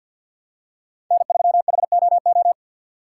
Morse code, heard as a keyed tone, about 53 wpm, 700 Hz. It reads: N3HOO
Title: USA Call Signs - 5 Characters - 2x Character Spacing - 50wpm